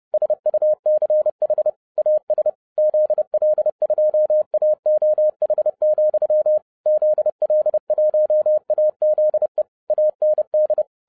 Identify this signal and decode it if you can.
Morse code, heard as a keyed tone, about 30 wpm, 610 Hz. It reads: SUCH AS ZL2AOH, ZL1AZE AND